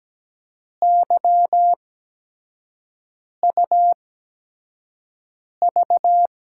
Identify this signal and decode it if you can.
Morse code, heard as a keyed tone, about 17 wpm, spaced as for 5 wpm, 700 Hz. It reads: YUV
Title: USA Call Signs - 5 Characters - 8x Character Spacing - 17wpm